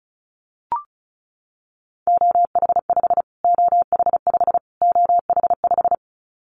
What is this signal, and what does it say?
Morse code, heard as a keyed tone, about 34 wpm, 700 Hz. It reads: OH5 OH5 OH5
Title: Callsign Prefixes for Top 25 DXCC - Repeated 3x - 35wpm